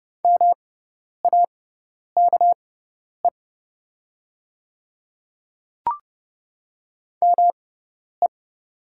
Morse code, heard as a keyed tone, about 30 wpm, spaced as for 10 wpm, 690 Hz. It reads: MAKE ME
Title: 6x Character Spacing - Top 500 Words - 30wpm